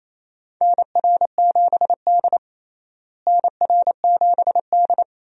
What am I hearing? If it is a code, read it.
Morse code, heard as a keyed tone, about 28 wpm, 700 Hz. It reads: NR7D NR7D
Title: USA Call Signs - 4 Characters - Repeated 3x - 28wpm